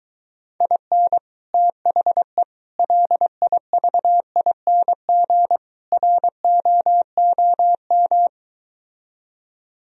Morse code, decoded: IN THE LIVING ROOM

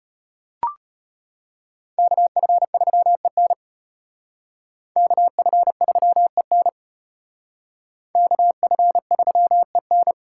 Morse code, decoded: KF3EN KF3EN KF3EN

